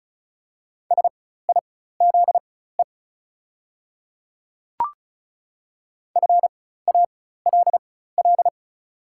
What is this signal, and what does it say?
Morse code, heard as a keyed tone, about 36 wpm, spaced as for 16 wpm, 690 Hz. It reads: SIZE FALL
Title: 4x Character Spacing - Top 500 Words - 35wpm